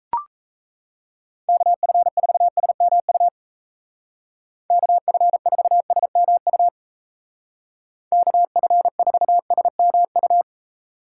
Morse code, heard as a keyed tone, about 35 wpm, 700 Hz. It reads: KF4SMU KF4SMU KF4SMU